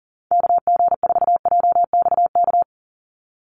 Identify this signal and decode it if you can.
Morse code, heard as a keyed tone, about 40 wpm, 690 Hz. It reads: KG4JXK
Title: USA Call Signs 40wpm